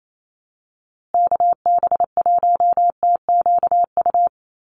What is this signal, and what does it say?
Morse code, heard as a keyed tone, about 28 wpm, 700 Hz. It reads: KB1TQU
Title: USA Call Signs Repeated 2x 28wpm